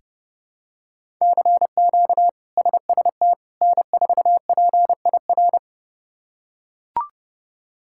Morse code, heard as a keyed tone, about 30 wpm, 700 Hz. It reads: CQ SST N4PIR
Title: K1USN SST QSO Practice - 30wpm